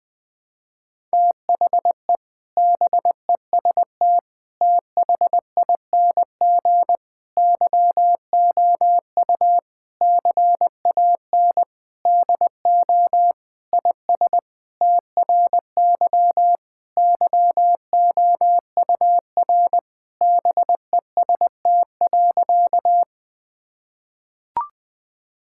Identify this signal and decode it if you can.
Morse code, heard as a keyed tone, about 20 wpm, 700 Hz. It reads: THE BEST THING YOU CAN DO IS TRY YOUR BEST.
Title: Sentences with 7-Word Phrases - 20wpm